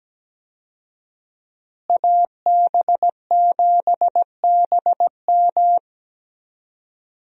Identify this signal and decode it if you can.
Morse code, heard as a keyed tone, about 17 wpm, 700 Hz. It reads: AB7BM
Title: USA Call Signs - 5 Characters - Repeated 3x - 17wpm